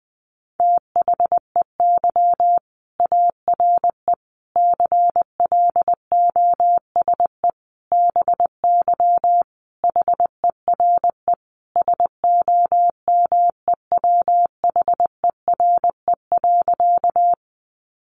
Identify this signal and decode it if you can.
Morse code, heard as a keyed tone, about 20 wpm, 700 Hz. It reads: THEY ARE CLOSE BY HERE SOMEWHERE.
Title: Sentences from Top 1400 Words 20wpm